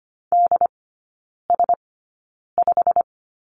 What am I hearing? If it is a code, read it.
Morse code, heard as a keyed tone, about 25 wpm, spaced as for 9 wpm, 690 Hz. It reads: DS5
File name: Lesson 15e - Number 5 - Three Characters 25wpm